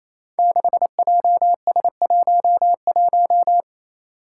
Morse code, heard as a keyed tone, about 28 wpm, 700 Hz. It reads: 6JS11
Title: Five Challenging Letter-Number Combinations 28wpm